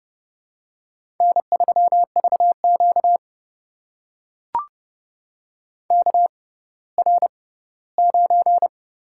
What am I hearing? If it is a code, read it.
Morse code, heard as a keyed tone, about 31 wpm, spaced as for 17 wpm, 700 Hz. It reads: N3VQ KR9
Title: USA Call Signs - 4 Characters - 6x Character Spacing - 30wpm